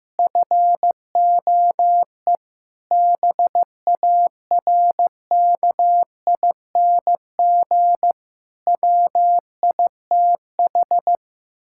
Morse code, heard as a keyed tone, about 15 wpm, 700 Hz. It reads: FOE BARKING WITH
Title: Pangram Challenge v2 - 15wpm